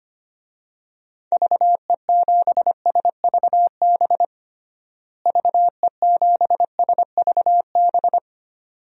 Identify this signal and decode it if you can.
Morse code, heard as a keyed tone, about 25 wpm, 700 Hz. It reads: VE7SVB VE7SVB
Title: DXCC-100 Call Signs - 6 Characters - Repeated 2x - 25wpm